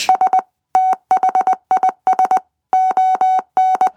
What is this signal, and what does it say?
Morse code, heard as a keyed tone, about 20 wpm, 750 Hz. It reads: S THIS ON